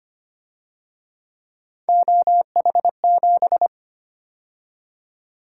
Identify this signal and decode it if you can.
Morse code, heard as a keyed tone, about 29 wpm, 700 Hz. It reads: OH7
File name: Speed-Racing - Three Letter-Number Combinations - 25wpm